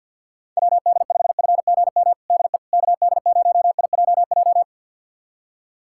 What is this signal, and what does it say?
Morse code, heard as a keyed tone, about 47 wpm, 700 Hz. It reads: WD5FGM DE KD0IJJ
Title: USA Callsign DE Callsign - 6 Characters - 50wpm